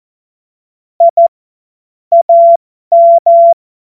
Morse code, heard as a keyed tone, about 14 wpm, spaced as for 12 wpm, 680 Hz. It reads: I AM
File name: Morsekurs / Witze